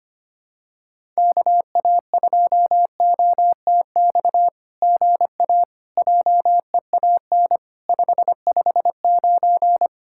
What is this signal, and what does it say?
Morse code, heard as a keyed tone, about 25 wpm, 700 Hz. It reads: KA2OTX GA JEAN 559
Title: SOTA QSO Practice 25wpm